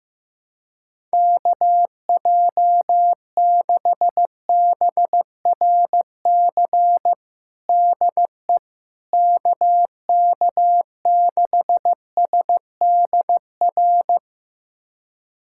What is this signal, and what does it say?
Morse code, heard as a keyed tone, about 15 wpm, 700 Hz. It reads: KJ6BRC DE KK6SDR
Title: USA Callsign DE Callsign - 6 Characters - 15wpm